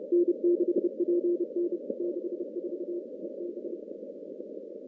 Morse code, heard as a keyed tone, about 30 wpm, 350 Hz. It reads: N6PN 64 EB
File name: contests / CW-SS-k5tr- -2200